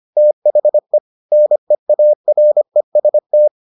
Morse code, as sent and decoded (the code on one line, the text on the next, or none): - .... / -. .- .-. ... -
THE NEAREST